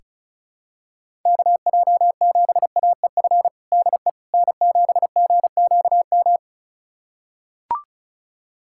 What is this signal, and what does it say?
Morse code, heard as a keyed tone, about 36 wpm, 700 Hz. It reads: KJ7AEF DE N7GQM
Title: USA Callsign DE Callsign 35wpm